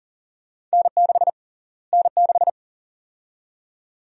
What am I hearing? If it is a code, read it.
Morse code, heard as a keyed tone, about 40 wpm, 700 Hz. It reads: N6 N6
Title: USA Call Sign Prefixes - Repeated 2x - 40wpm